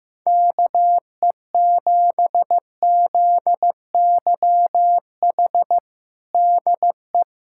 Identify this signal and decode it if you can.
Morse code, heard as a keyed tone, about 15 wpm, 700 Hz. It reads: KE7ZYH DE
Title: USA Callsign DE Callsign - 6 Characters - 15wpm